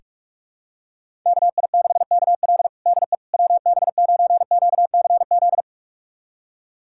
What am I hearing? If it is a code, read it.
Morse code, heard as a keyed tone, about 42 wpm, 700 Hz. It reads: KI6KL DE WB9QCZ
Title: USA Callsign DE Callsign 45wpm